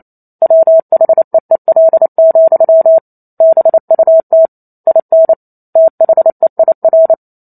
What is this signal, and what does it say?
Morse code, heard as a keyed tone, about 28 wpm, 650 Hz. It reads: WHEEL, BUT IN THEIR
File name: Chapter 2 - 28 WPM